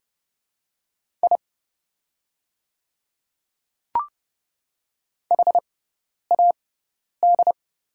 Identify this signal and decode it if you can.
Morse code, heard as a keyed tone, about 30 wpm, spaced as for 10 wpm, 700 Hz. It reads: I HAD
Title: 6x Character Spacing - Top 100 Words - 30wpm